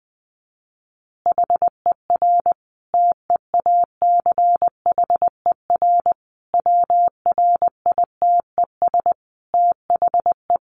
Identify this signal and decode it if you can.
Morse code, heard as a keyed tone, about 20 wpm, 700 Hz. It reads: HER TEACHER WRITES THE
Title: Additional Easy Phrases 20wpm